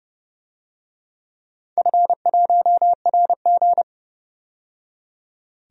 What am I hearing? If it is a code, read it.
Morse code, heard as a keyed tone, about 32 wpm, 700 Hz. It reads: F1RG
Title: DXCC-100 Call Signs - 4 Characters - 30wpm